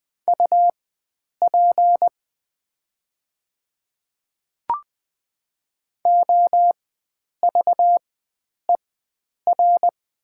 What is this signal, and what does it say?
Morse code, heard as a keyed tone, about 20 wpm, spaced as for 9 wpm, 700 Hz. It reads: UP OVER
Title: 4x Character Spacing - Top 100 Words - 20wpm